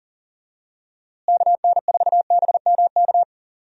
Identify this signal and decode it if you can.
Morse code, heard as a keyed tone, about 38 wpm, 700 Hz. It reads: KN4BMK
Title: USA Call Signs - 6 Characters - Repeated 3x - 40wpm